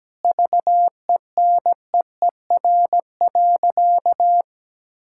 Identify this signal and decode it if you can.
Morse code, heard as a keyed tone, about 17 wpm, 700 Hz. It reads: VENEER.
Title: Pangram Challenge v2 - 17wpm